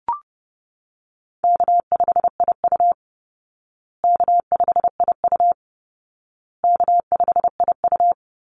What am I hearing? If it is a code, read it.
Morse code, heard as a keyed tone, about 30 wpm, 700 Hz. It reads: K5IU K5IU K5IU
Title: Easy Call Signs Repeated 3x - 30wpm